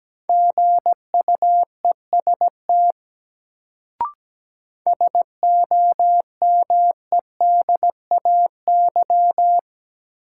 Morse code, heard as a keyed tone, about 17 wpm, 700 Hz. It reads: GUEST SOMEDAY
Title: Sentences from Top 1600 Words - New Word Drill - 17wpm